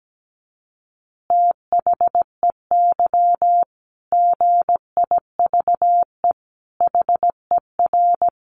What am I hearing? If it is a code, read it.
Morse code, heard as a keyed tone, about 17 wpm, 700 Hz. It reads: THEY GIVE HER
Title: Easy Phrases 17wpm